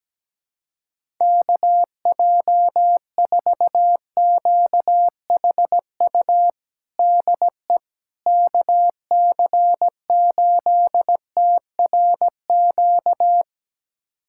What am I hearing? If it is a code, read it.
Morse code, heard as a keyed tone, about 17 wpm, 700 Hz. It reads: KJ4QHU DE KC8TRQ
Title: USA Callsign DE Callsign 17wpm